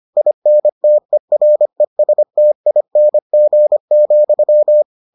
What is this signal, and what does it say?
Morse code, heard as a keyed tone, about 25 wpm, 600 Hz. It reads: INTERESTING,